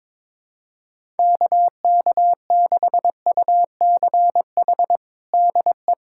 Morse code, heard as a keyed tone, about 22 wpm, 700 Hz. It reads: KK6UCH DE
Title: USA Callsign DE Callsign 22wpm